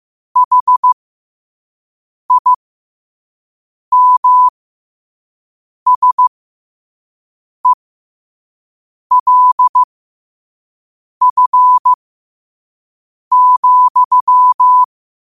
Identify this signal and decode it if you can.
Morse code, heard as a keyed tone, about 15 wpm, spaced as for 5 wpm, 1000 Hz. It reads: HIMSELF,
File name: qotdfiles / 5wpm